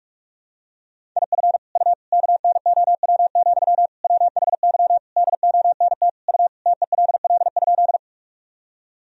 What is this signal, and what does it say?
Morse code, heard as a keyed tone, about 46 wpm, 700 Hz. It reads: IF U KNOW, WHY DONT U TELL?
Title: Sentences from Top 300 Words and Abbreviations 45wpm